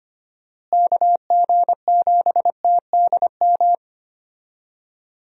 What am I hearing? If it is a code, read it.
Morse code, heard as a keyed tone, about 25 wpm, 700 Hz. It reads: KG7TDM
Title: USA Call Signs - 6 Characters - Repeated 2x - 25wpm